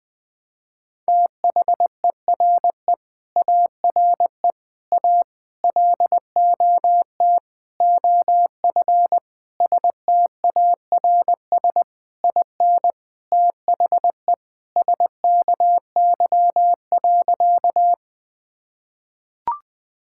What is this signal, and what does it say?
Morse code, heard as a keyed tone, about 20 wpm, 700 Hz. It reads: THERE ARE A LOT OF STARS IN THE SKY.
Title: Sentences with 4-Word Phrases - 20wpm